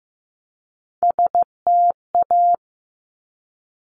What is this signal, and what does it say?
Morse code, heard as a keyed tone, about 15 wpm, 700 Hz. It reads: STA
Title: Top Three Letter Combinations in Words 15wpm